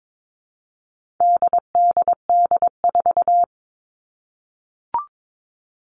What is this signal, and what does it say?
Morse code, heard as a keyed tone, about 22 wpm, 700 Hz. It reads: DDD4
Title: All Cut Numbers - 4-Digit 22wpm